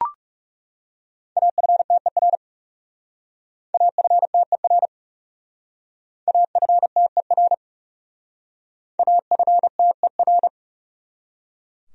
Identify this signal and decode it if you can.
Morse code, heard as a keyed tone, about 39 wpm, 700 Hz. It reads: AFTER AFTER AFTER AFTER